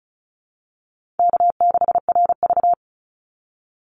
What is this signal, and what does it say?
Morse code, heard as a keyed tone, about 34 wpm, 700 Hz. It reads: K6RV
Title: Easy Call Signs Repeated 2x - 35wpm